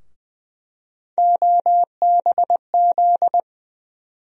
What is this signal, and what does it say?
Morse code, heard as a keyed tone, about 20 wpm, 700 Hz. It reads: OBZ
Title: USA Call Sign Postfixes - 3 Letters 20wpm